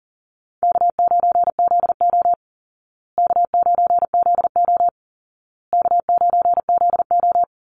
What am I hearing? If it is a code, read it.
Morse code, heard as a keyed tone, about 40 wpm, 700 Hz. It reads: K9ZO K9ZO K9ZO
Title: Easy Call Signs Repeated 3x - 40wpm